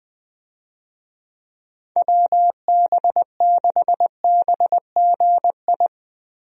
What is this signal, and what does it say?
Morse code, heard as a keyed tone, about 21 wpm, 700 Hz. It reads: WB6BGI
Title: USA Call Signs - 6 Characters - Repeated 2x - 20wpm